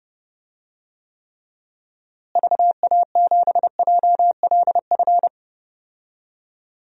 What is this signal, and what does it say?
Morse code, heard as a keyed tone, about 30 wpm, 700 Hz. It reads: VA7JLF
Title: DXCC-100 Call Signs - 6 Characters - Repeated 2x - 30wpm